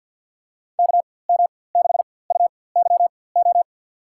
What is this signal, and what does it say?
Morse code, heard as a keyed tone, about 46 wpm, spaced as for 23 wpm, 700 Hz. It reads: KM6UYO
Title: USA Call Signs - 6 Characters - 4x Character Spacing - 50wpm